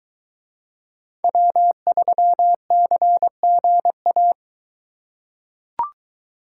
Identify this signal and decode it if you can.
Morse code, heard as a keyed tone, about 22 wpm, 700 Hz. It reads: W3CGA